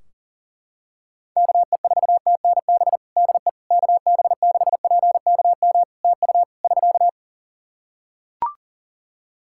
Morse code, heard as a keyed tone, about 40 wpm, 700 Hz. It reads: KE4TNB DE KB6PKM TU <SK>
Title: Practice QSO - Exchange 4 - Ending - 40wpm